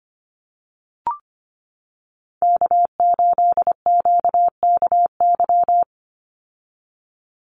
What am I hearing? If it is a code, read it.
Morse code, heard as a keyed tone, about 25 wpm, 700 Hz. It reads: K8QKY
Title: Call Signs - 25wpm